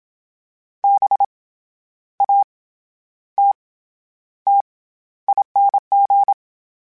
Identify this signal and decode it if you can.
Morse code, heard as a keyed tone, about 27 wpm, 800 Hz. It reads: B A T T ING
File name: Special INGs-mp3s / Special INGs-mp3s / SlowToFaster ing @25